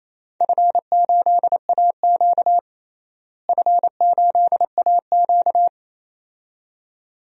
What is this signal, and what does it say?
Morse code, heard as a keyed tone, about 28 wpm, 700 Hz. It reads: F8AQ F8AQ
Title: DXCC-100 Call Signs - 4 Characters - Repeated 2x - 28wpm